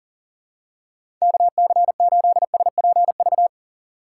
Morse code, heard as a keyed tone, about 40 wpm, 700 Hz. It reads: KC8SPV
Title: USA Call Signs - 6 Characters - 4x Character Spacing - 40wpm